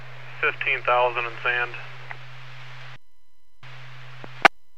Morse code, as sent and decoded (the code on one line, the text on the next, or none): none